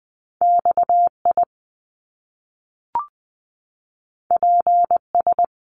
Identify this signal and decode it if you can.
Morse code, heard as a keyed tone, about 20 wpm, 700 Hz. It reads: X I P S
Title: Top Two Letter Combinations in Words 20wpm